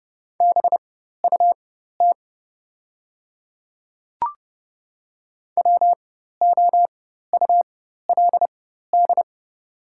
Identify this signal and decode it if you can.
Morse code, heard as a keyed tone, about 30 wpm, spaced as for 14 wpm, 690 Hz. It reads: BUT WOULD
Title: 4x Character Spacing - Top 100 Words - 30wpm